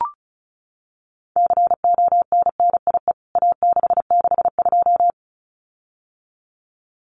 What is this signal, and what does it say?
Morse code, heard as a keyed tone, about 34 wpm, 700 Hz. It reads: CONNIE A662
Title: CWT Exchange 35wpm